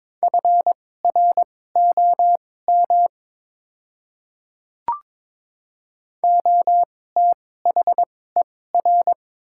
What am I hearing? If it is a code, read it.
Morse code, heard as a keyed tone, about 22 wpm, spaced as for 16 wpm, 700 Hz. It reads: FROM OTHER
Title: 2x Character Spacing - Top 200 Words - 22wpm